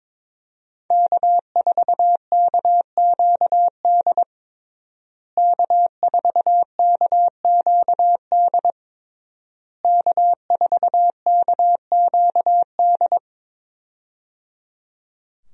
Morse code, heard as a keyed tone, about 22 wpm, 700 Hz. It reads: K4KQD K4KQD K4KQD
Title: USA Call Signs - 5 Characters - Repeated 3x - 22wpm